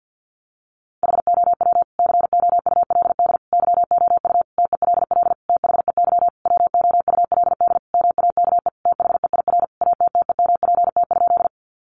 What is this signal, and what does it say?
Morse code, heard as a keyed tone, about 50 wpm, 700 Hz. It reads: HOW COULD YOU TELL THEY WOULD MAKE THEIR ATTEMPT?
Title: Sentences from Top 1000 Words 50wpm